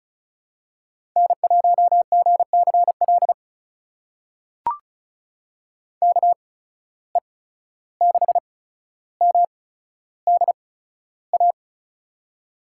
Morse code, heard as a keyed tone, about 36 wpm, spaced as for 14 wpm, 700 Hz. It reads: N1GCL KE6MDA
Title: USA Call Signs - 8x Character Spacing - 35wpm